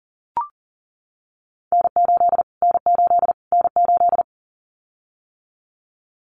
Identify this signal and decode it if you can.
Morse code, heard as a keyed tone, about 40 wpm, 700 Hz. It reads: N8 N8 N8
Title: Callsign Prefixes for Top 25 DXCC - Repeated 3x - 40wpm